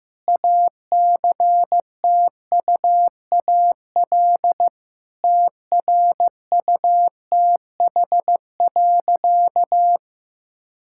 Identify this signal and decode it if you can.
Morse code, heard as a keyed tone, about 15 wpm, 700 Hz. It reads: ACTUAL TRUTH.
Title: Sentences from Top 1000 Words - Review Part 5 - 15wpm